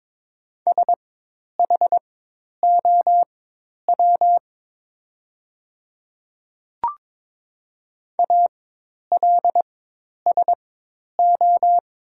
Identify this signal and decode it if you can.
Morse code, heard as a keyed tone, about 22 wpm, spaced as for 10 wpm, 700 Hz. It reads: SHOW ALSO
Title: 4x Character Spacing - Top 200 Words - 22wpm